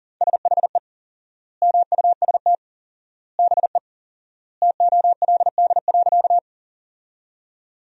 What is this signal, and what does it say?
Morse code, heard as a keyed tone, about 40 wpm, 700 Hz. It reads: SHE MUST BE TOLD.
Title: 4x Word-Spacing - Sentences from Top 400 Words - 40wpm